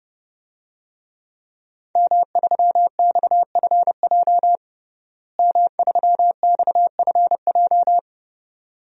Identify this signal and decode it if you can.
Morse code, heard as a keyed tone, about 32 wpm, 700 Hz. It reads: M3XFJ M3XFJ
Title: DXCC-100 Call Signs - 5 Characters - Repeated 2x - 30wpm